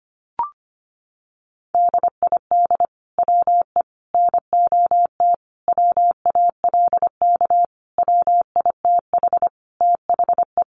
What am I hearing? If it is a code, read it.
Morse code, heard as a keyed tone, about 25 wpm, 700 Hz. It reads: DID WE NOT WALK WITH THE
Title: Sentences from Top 1400 Words 25wpm